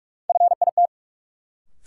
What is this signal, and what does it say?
Morse code, heard as a keyed tone, about 46 wpm, 700 Hz. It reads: FIT
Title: Sentences from Top 800 Words - New Word Drill - 45wpm